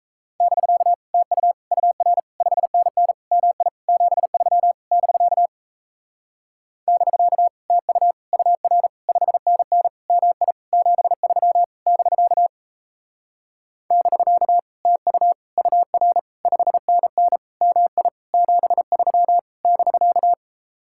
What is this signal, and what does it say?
Morse code, heard as a keyed tone, about 37 wpm, 700 Hz. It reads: <BK> TU UR 5NN MI 73 <BK> <BK> TU UR 5NN MI 73 <BK> <BK> TU UR 5NN MI 73 <BK>